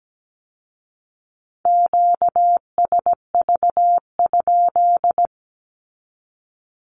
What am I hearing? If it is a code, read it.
Morse code, heard as a keyed tone, about 17 wpm, 700 Hz. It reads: QSV?
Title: Q-Codes with Spoken Meaning - 17wpm